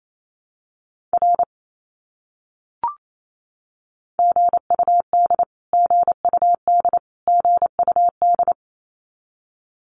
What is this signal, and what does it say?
Morse code, heard as a keyed tone, about 28 wpm, 700 Hz. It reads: R GUD GUD GUD
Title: Abbreviations Repeated 3x - 28wpm